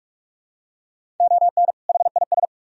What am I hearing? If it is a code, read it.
Morse code, heard as a keyed tone, about 44 wpm, 700 Hz. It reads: ON HIS